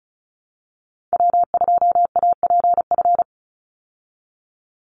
Morse code, heard as a keyed tone, about 34 wpm, 700 Hz. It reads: W2APF
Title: Call Signs Repeated 3x - 35wpm